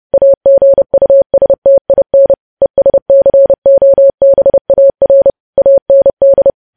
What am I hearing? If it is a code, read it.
Morse code, heard as a keyed tone, about 30 wpm, 550 Hz. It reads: AGUSTIN ESCOBAR AND
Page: news headlines in morse code at 30 wpm